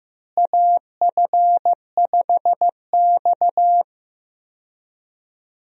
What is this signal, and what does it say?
Morse code, heard as a keyed tone, about 16 wpm, 700 Hz. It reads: AF5X
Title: USA Call Signs - 4 Characters - Repeated 3x - 15wpm